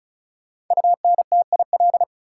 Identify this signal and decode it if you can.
Morse code, heard as a keyed tone, about 36 wpm, 700 Hz. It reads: UNTIL
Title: Sentences from Top 500 Words - New Word Drill - 35wpm